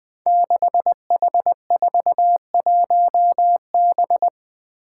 Morse code, heard as a keyed tone, about 20 wpm, 700 Hz. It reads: 6H41B
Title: Five Challenging Letter-Number Combinations 20wpm